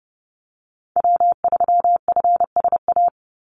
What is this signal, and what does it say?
Morse code, heard as a keyed tone, about 30 wpm, 700 Hz. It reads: W3FSA
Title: Call Signs Repeated 3x - 30wpm